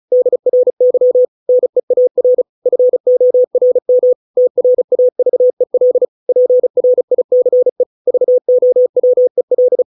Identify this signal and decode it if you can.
Morse code, heard as a keyed tone, about 35 wpm, 500 Hz. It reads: DRY NEAR FORM TRAVEL PRICE VOWEL